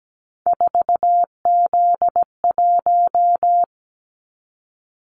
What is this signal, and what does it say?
Morse code, heard as a keyed tone, about 17 wpm, 700 Hz. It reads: 4Z1
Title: Callsign Prefixes for Top 50 DXCC - 17wpm